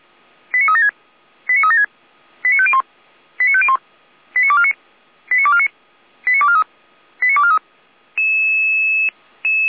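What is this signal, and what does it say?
Five-tone selective call (ZVEI), heard as quick runs of five tones, tones of about 70 ms. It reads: 89376 89376 89501 89501 89249 89249 89243 89243
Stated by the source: ZVEI five-tone selective call 89240 received by SDR on channel 455, from feuerwehr-orlamuende.org fire brigade audio archive.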